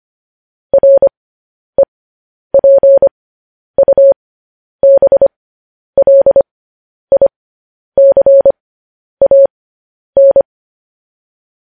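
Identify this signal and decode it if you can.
Morse code, heard as a keyed tone, about 25 wpm, spaced as for 10 wpm, 550 Hz. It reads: REPUBLICAN
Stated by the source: Morse code news headlines sent as a reading transcription, preceded by vvv.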